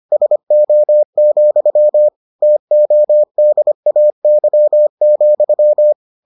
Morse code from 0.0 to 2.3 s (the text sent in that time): SO,